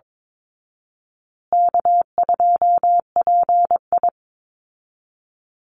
Morse code, heard as a keyed tone, about 22 wpm, 700 Hz. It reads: K2PI
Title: Easy Call Signs Repeated 3x - 22wpm